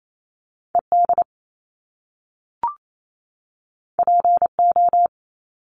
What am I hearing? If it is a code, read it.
Morse code, heard as a keyed tone, about 28 wpm, 700 Hz. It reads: ED PO
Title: Top Two Letter Combinations in Words 28wpm